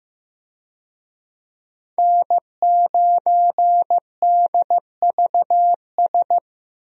Morse code, heard as a keyed tone, about 15 wpm, 700 Hz. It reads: N9DVS
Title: USA Call Signs - 5 Characters - 15wpm